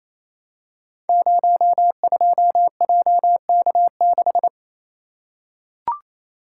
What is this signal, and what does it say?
Morse code, heard as a keyed tone, about 29 wpm, 700 Hz. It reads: 02JK6
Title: Code Groups - 5 Letter:Numbers - 28wpm